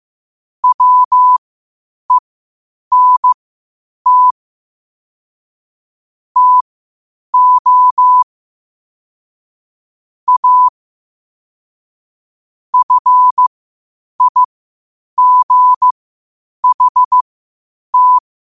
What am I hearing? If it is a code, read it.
Morse code, heard as a keyed tone, about 15 wpm, spaced as for 8 wpm, 1000 Hz. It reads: WENT TO A FIGHT
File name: qotdfiles / 8wpm